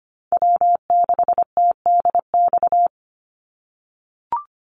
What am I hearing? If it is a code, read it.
Morse code, heard as a keyed tone, about 25 wpm, 700 Hz. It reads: W6TDX
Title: NA SOTA Chaser Callsigns 25wpm